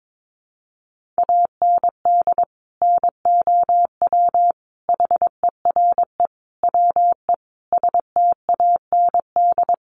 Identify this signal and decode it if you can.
Morse code, heard as a keyed tone, about 22 wpm, 700 Hz. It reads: AND NOW HERE WE STAND
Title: Sentences from Top 1400 Words 22wpm